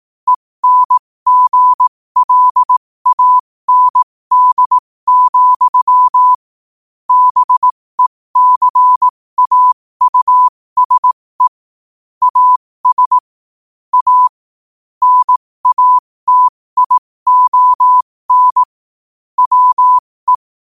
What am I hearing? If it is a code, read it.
Morse code, heard as a keyed tone, about 18 wpm, spaced as for 15 wpm, 1000 Hz. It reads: ENGLAND, BECAUSE AS A NATION WE
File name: qotdfiles / 15wpm